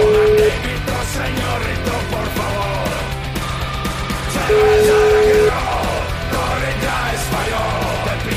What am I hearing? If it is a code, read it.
Ringback tone (Europe, 425 Hz), with noise.